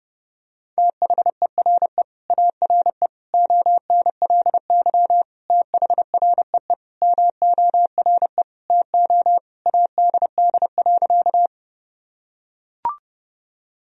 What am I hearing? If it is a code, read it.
Morse code, heard as a keyed tone, about 30 wpm, 700 Hz. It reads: THERE ARE ONLY THREE MORE TO ADD.
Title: Sentences from Top 1000 Words - Review Part 1 - 30wpm